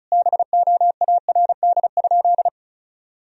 Morse code, heard as a keyed tone, about 34 wpm, 700 Hz. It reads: BOARD?
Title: Sentences from Top 1000 Words - Review Part 4 - 35wpm